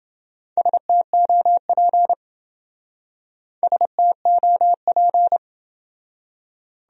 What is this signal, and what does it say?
Morse code, heard as a keyed tone, about 28 wpm, 700 Hz. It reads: STOP STOP